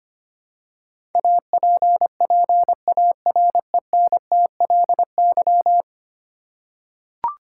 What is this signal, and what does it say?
Morse code, heard as a keyed tone, about 25 wpm, 700 Hz. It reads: APPARENTLY
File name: practice / 25wpm / Ten Letter Words - 25wpm